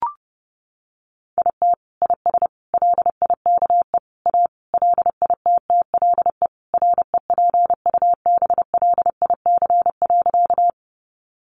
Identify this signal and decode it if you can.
Morse code, heard as a keyed tone, about 30 wpm, 700 Hz. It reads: IT IS LIKE A LITTLE REPUBLIC.
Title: Sentences from Top 1500 Words 30wpm